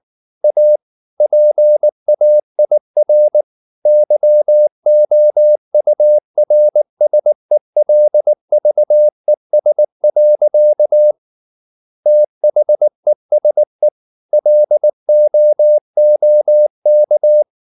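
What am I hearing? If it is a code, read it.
Morse code, heard as a keyed tone, about 19 wpm, 600 Hz. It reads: A PAIR YOURSELVES. THESE LOOK
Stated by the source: Morse code at 19 wpm from CW med Gnister, reading Ken Follett's novel Never in English.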